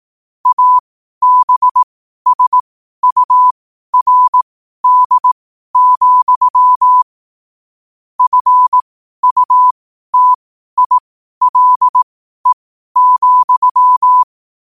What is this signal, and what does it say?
Morse code, heard as a keyed tone, about 18 wpm, spaced as for 12 wpm, 1000 Hz. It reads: ABSURD, FUTILE,
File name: qotdfiles / 12wpm